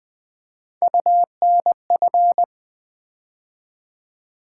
Morse code, heard as a keyed tone, about 20 wpm, 700 Hz. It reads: UNF
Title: Build A Word - 3 Letters At A Time - 6-Letter Words - 20wpm